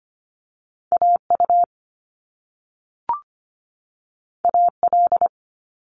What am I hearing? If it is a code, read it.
Morse code, heard as a keyed tone, about 25 wpm, 700 Hz. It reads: AU AL
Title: Top Two Letter Combinations in Words 25wpm